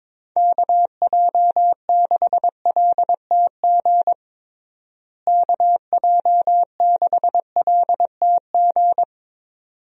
Morse code, heard as a keyed tone, about 22 wpm, 700 Hz. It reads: KJ6LTG KJ6LTG
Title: USA Call Signs - 6 Characters - Repeated 2x - 22wpm